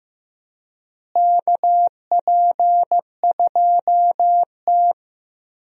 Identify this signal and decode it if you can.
Morse code, heard as a keyed tone, about 15 wpm, 700 Hz. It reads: KP2T
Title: USA Call Signs - 4 Characters - Repeated 3x - 15wpm